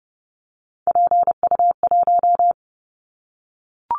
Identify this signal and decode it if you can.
Morse code, heard as a keyed tone, about 30 wpm, 700 Hz. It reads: PU1
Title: Callsign Prefixes for Top 50 DXCC - 30wpm